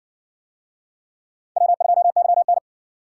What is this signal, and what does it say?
Morse code, heard as a keyed tone, about 60 wpm, 700 Hz. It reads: W2YN